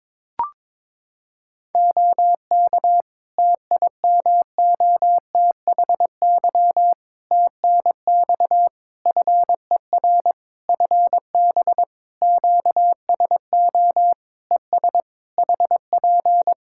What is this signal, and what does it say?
Morse code, heard as a keyed tone, about 22 wpm, 700 Hz. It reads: OK TIMOTHY TNX FER FB QSO ES HP
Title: K1USN SST QSO Practice 22wpm